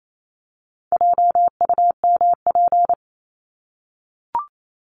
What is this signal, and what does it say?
Morse code, heard as a keyed tone, about 28 wpm, 700 Hz. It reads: JUMP
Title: Top 1251-1500 Words 28wpm